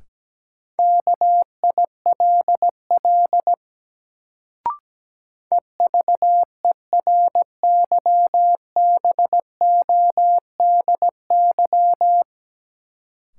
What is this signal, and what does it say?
Morse code, heard as a keyed tone, about 17 wpm, 700 Hz. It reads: KILL EVERYBODY
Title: Sentences from Top 1400 Words - New Word Drill Alt - 17wpm